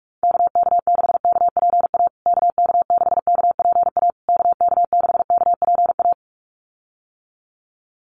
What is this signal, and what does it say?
Morse code, heard as a keyed tone, about 44 wpm, 700 Hz. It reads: KK6KPA KK6KPA KK6KPA
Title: USA Call Signs Repeated 3x 45wpm